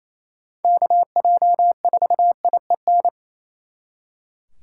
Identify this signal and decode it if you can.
Morse code, heard as a keyed tone, about 36 wpm, 700 Hz. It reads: KJ4IEN